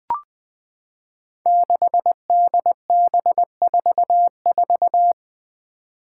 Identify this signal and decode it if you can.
Morse code, heard as a keyed tone, about 21 wpm, 700 Hz. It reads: 6DB44
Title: Five Challenging Letter-Number Combinations 20wpm